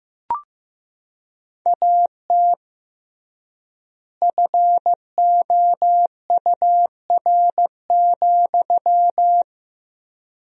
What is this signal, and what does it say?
Morse code, heard as a keyed tone, about 15 wpm, 700 Hz. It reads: AT FOUR,
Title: 3x Word-Spacing - Sentences from Top 300 Words - 15wpm